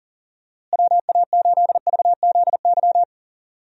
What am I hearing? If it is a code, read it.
Morse code, heard as a keyed tone, about 38 wpm, 700 Hz. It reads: WA8VZY
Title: USA Call Signs - 6 Characters - Repeated 3x - 40wpm